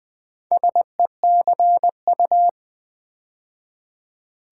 Morse code, heard as a keyed tone, about 20 wpm, 700 Hz. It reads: SECU
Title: Build A Word - 2 Letters At A Time - 8-Letter Words - 20wpm